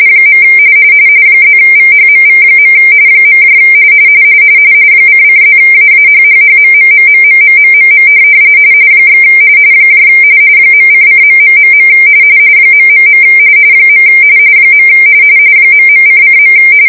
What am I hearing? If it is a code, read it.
RTTY, about 45.45 baud. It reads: . +VHMV. ::+AAHHHMMMMMMMMM / .::.... ... . ..::.